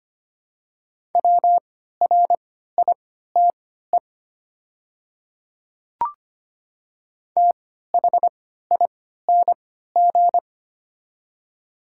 Morse code, heard as a keyed tone, about 25 wpm, spaced as for 14 wpm, 700 Hz. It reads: WRITE THING
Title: 3x Character Spacing - Top 100 Words - 25wpm